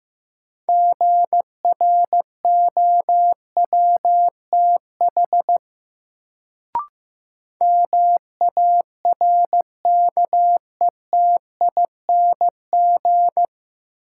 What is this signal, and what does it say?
Morse code, heard as a keyed tone, about 15 wpm, 700 Hz. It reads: GROWTH MARKETING
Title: Sentences from Top 1000 Words - Review Part 5 - New Word Drill - 15wpm